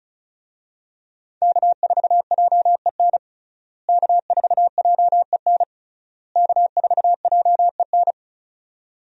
Morse code, heard as a keyed tone, about 36 wpm, 700 Hz. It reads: K4JEN K4JEN K4JEN
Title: USA Call Signs - 5 Characters - Repeated 3x - 35wpm